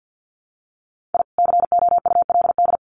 Morse code, heard as a keyed tone, about 48 wpm, 700 Hz. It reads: I COULD
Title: Sets of 5 Words Encore 50wpm